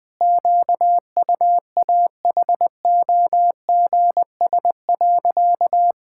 Morse code, heard as a keyed tone, about 20 wpm, 700 Hz. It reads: QUAHOGS.
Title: Pangram Challenge v2 - 20wpm